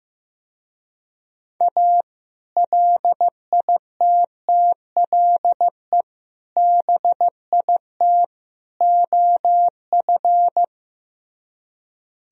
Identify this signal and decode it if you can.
Morse code, heard as a keyed tone, about 17 wpm, 700 Hz. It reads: A LITTLE BIT OF
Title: Speed-Racing - Sets of 4 Words 15wpm